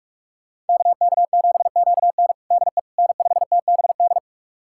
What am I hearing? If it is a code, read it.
Morse code, heard as a keyed tone, about 44 wpm, 700 Hz. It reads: KK7QN DE N5TBD
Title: USA Callsign DE Callsign - 5 Characters - 45wpm